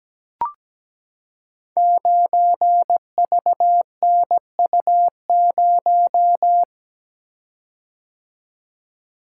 Morse code, heard as a keyed tone, about 17 wpm, 700 Hz. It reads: 9VNU0
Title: Code Groups - 5 Letter:Numbers - 17wpm